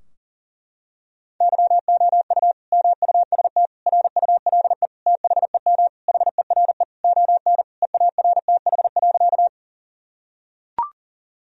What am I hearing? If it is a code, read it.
Morse code, heard as a keyed tone, about 40 wpm, 700 Hz. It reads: YOU MUST RULE THEM HERE ON EARTH.
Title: Sentences from Top 1000 Words - Review Part 2 - 40wpm